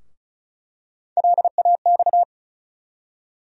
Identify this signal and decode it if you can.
Morse code, heard as a keyed tone, about 34 wpm, 700 Hz. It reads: LAX
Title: USA Call Sign Postfixes - 3 Letters 35wpm